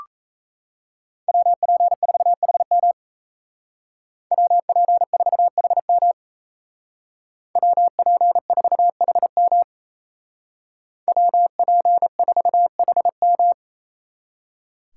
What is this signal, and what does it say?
Morse code, heard as a keyed tone, about 33 wpm, 700 Hz. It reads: WP4HM WP4HM WP4HM WP4HM
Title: Speed Racing - USA 2x2 Call Signs 28wpm